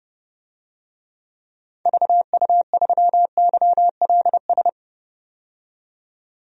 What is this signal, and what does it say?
Morse code, heard as a keyed tone, about 32 wpm, 700 Hz. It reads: VU3YLS
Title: DXCC-100 Call Signs - 6 Characters - 30wpm